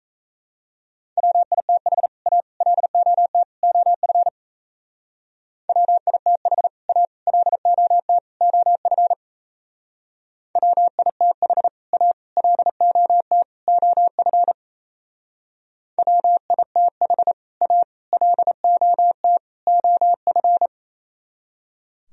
Morse code, heard as a keyed tone, about 34 wpm, 700 Hz. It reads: WITH A LOT OF WITH A LOT OF WITH A LOT OF WITH A LOT OF
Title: Speed-Racing - Sets of 4 Words 28wpm